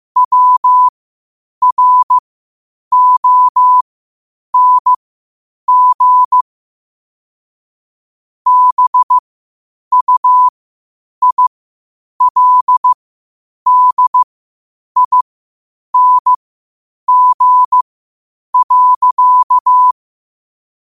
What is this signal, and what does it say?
Morse code, heard as a keyed tone, about 15 wpm, spaced as for 8 wpm, 1000 Hz. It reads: WRONG BUILDING.
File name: qotdfiles / 8wpm